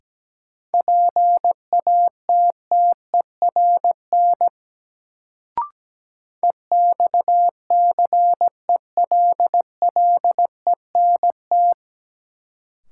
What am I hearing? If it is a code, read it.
Morse code, heard as a keyed tone, about 17 wpm, 700 Hz. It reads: PATTERN EXCELLENT
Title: Sentences from Top 1000 Words - New Word Drill - 17wpm